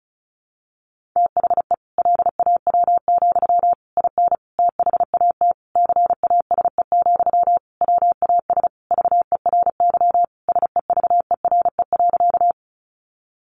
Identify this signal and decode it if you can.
Morse code, heard as a keyed tone, about 34 wpm, 700 Hz. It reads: THE LAW, IN THAT CASE, WAS VERY SEVERE.
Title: Sentences from Top 1400 Words 35wpm